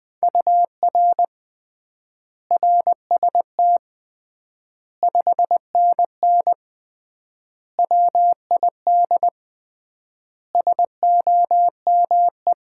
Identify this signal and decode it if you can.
Morse code, heard as a keyed tone, about 20 wpm, 700 Hz. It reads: UR RST 5NN WID SOME
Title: Common QSO Phrases - 3x Word-Spacing - 20wpm